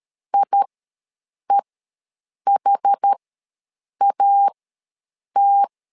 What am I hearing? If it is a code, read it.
Morse code, heard as a keyed tone, about 13 wpm, spaced as for 7 wpm, 800 Hz. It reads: IEHAT
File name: LEZIONI / 2 UHDN / 2 UHD ws 12wpm 1722